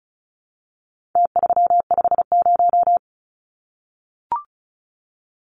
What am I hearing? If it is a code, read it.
Morse code, heard as a keyed tone, about 34 wpm, 700 Hz. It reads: T350
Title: Numbers - Contest Style - 4-Digit 35wpm